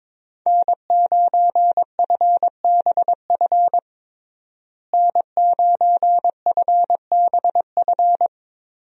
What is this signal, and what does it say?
Morse code, heard as a keyed tone, about 22 wpm, 700 Hz. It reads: N9FBF N9FBF
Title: USA Call Signs - 5 Characters - Repeated 2x - 22wpm